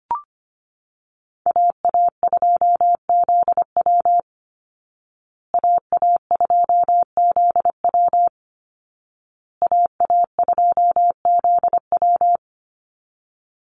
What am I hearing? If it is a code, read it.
Morse code, heard as a keyed tone, about 25 wpm, 700 Hz. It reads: AA2ZW AA2ZW AA2ZW
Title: Call Signs Repeated 2x - 25wpm